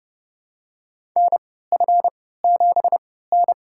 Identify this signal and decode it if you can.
Morse code, heard as a keyed tone, about 30 wpm, spaced as for 17 wpm, 700 Hz. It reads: NF7N
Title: USA Call Signs - 4 Characters - 3x Character Spacing - 30wpm